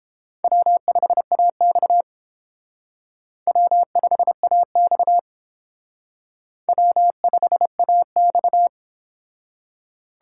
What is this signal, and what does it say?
Morse code, heard as a keyed tone, about 29 wpm, 700 Hz. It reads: W5AX W5AX W5AX